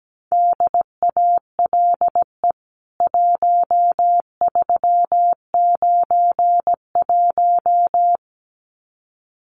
Morse code, heard as a keyed tone, about 17 wpm, 700 Hz. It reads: DALE 1391
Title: CWT Exchange 17wpm